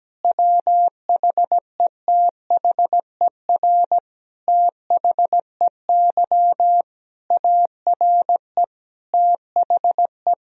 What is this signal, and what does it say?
Morse code, heard as a keyed tone, about 17 wpm, 700 Hz. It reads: WHETHER THEY ARE THE
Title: Sentences from Top 1000 Words - Review Part 4 - 17wpm